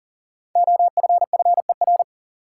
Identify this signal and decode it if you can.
Morse code, heard as a keyed tone, about 40 wpm, 700 Hz. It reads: OFFER
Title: Sentences from Top 900 Words - New Word Drill - 40wpm